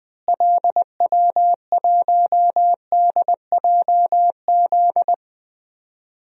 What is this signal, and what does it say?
Morse code, heard as a keyed tone, about 20 wpm, 700 Hz. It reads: LW1DJZ
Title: DXCC-100 Call Signs - 6 Characters - Repeated 2x - 20wpm